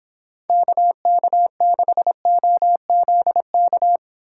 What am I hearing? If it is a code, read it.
Morse code, heard as a keyed tone, about 26 wpm, 700 Hz. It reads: KK6OZK